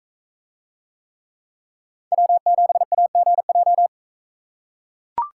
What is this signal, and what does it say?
Morse code, heard as a keyed tone, about 42 wpm, 700 Hz. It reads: W7AGJ